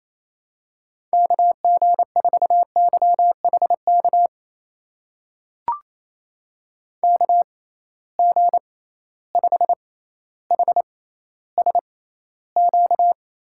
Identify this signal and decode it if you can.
Morse code, heard as a keyed tone, about 28 wpm, spaced as for 14 wpm, 700 Hz. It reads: KG4YHK KG5HSQ
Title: USA Call Signs - 6 Characters - 6x Character Spacing - 28wpm